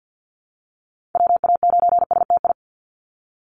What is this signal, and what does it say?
Morse code, heard as a keyed tone, about 50 wpm, 700 Hz. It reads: WA9STI
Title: NA SOTA Activator Callsigns 50wpm